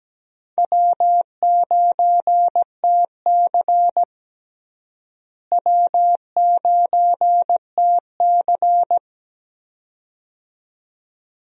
Morse code, heard as a keyed tone, about 17 wpm, 700 Hz. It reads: W9TC W9TC
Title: USA Call Signs - 4 Characters - Repeated 2x - 17wpm